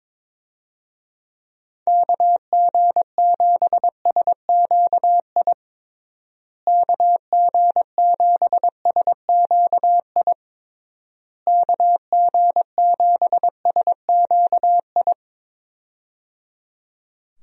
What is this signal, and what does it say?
Morse code, heard as a keyed tone, about 22 wpm, 700 Hz. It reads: KG7SQI KG7SQI KG7SQI
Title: USA Call Signs - 6 Characters - Repeated 3x - 22wpm